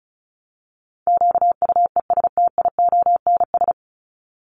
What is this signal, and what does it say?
Morse code, heard as a keyed tone, about 34 wpm, 700 Hz. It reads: QUESTIONS
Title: ES:S Suffix - Top 100 Words with Base Words 35wpm